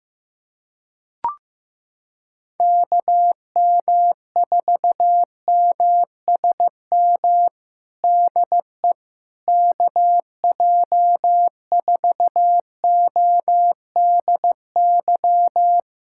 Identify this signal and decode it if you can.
Morse code, heard as a keyed tone, about 15 wpm, 700 Hz. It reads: KM4MSM DE KJ4ODY
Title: USA Callsign DE Callsign 15wpm